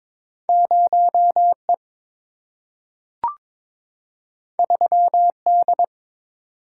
Morse code, heard as a keyed tone, about 22 wpm, 700 Hz. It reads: 0E 3D
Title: Single-Letter Single-Number 22wpm